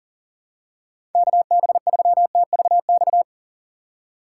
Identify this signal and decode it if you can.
Morse code, heard as a keyed tone, about 40 wpm, 700 Hz. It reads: KB3TVX
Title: USA Call Signs - 3x Character Spacing - 40wpm